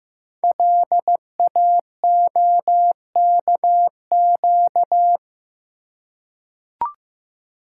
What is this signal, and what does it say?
Morse code, heard as a keyed tone, about 15 wpm, 700 Hz. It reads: LAOKQ
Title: Code Groups - 5 Letters - 15wpm